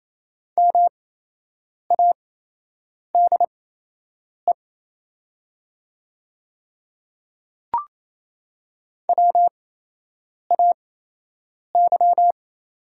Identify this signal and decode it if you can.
Morse code, heard as a keyed tone, about 27 wpm, spaced as for 8 wpm, 700 Hz. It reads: MADE WAY
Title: 8x Character Spacing - Top 200 Words - 28wpm